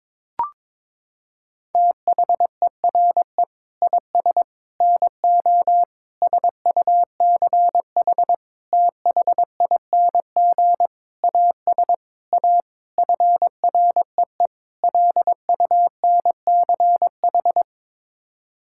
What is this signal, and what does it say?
Morse code, heard as a keyed tone, about 22 wpm, 700 Hz. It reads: THERE IS NO SUCH THING AS A FREE LUNCH
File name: Sets of 9 Words - 22wpm